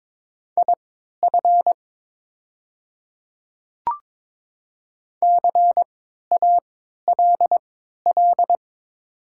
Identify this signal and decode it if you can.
Morse code, heard as a keyed tone, about 22 wpm, spaced as for 13 wpm, 700 Hz. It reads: IF CALL
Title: 3x Character Spacing - Top 100 Words - 22wpm